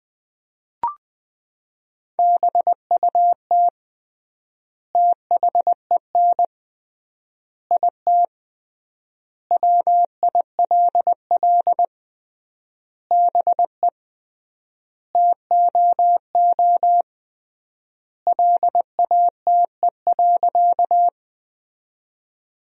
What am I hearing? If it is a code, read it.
Morse code, heard as a keyed tone, about 20 wpm, 700 Hz. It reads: BUT THEN IT WILL BE TOO LATE.
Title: 3x Word-Spacing - Sentences from Top 300 Words - 20wpm